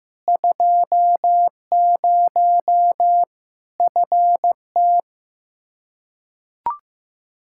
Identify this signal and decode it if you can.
Morse code, heard as a keyed tone, about 15 wpm, 700 Hz. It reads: 20 FT
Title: Practice QSO - Exchange 2 - Info - 15wpm